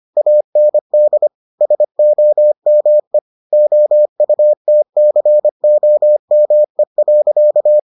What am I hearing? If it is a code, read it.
Morse code, heard as a keyed tone, about 25 wpm, 600 Hz. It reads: AND SOME OUTCOME.